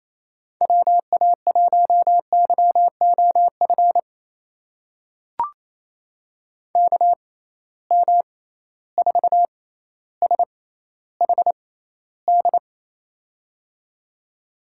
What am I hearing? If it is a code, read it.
Morse code, heard as a keyed tone, about 28 wpm, spaced as for 14 wpm, 700 Hz. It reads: WA1YOF KM4SHD
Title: USA Call Signs - 6 Characters - 6x Character Spacing - 28wpm